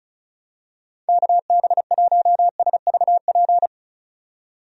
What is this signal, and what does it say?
Morse code, heard as a keyed tone, about 36 wpm, 700 Hz. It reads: KB1SVP
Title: USA Call Signs - 6 Characters - 35wpm